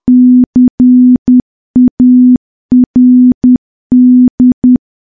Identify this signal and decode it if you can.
Morse code, heard as a keyed tone, about 10 wpm, 250 Hz. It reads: CARD
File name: morse / morse-training-10WPM